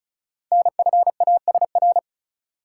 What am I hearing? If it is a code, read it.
Morse code, heard as a keyed tone, about 36 wpm, 700 Hz. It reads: NFASR
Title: Code Groups - 5 Letters - 35wpm